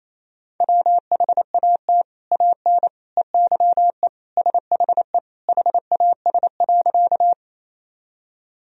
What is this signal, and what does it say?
Morse code, heard as a keyed tone, about 28 wpm, 700 Hz. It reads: WHAT AN EYE SHE HAS.
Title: Sentences from Top 1000 Words - Review Part 1 - 28wpm